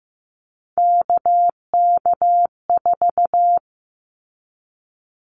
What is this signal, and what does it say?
Morse code, heard as a keyed tone, about 15 wpm, 700 Hz. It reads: KK4
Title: Callsign Prefixes for Top 25 DXCC - 15wpm